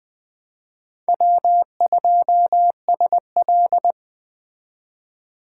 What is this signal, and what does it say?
Morse code, heard as a keyed tone, about 20 wpm, 700 Hz. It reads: W2SL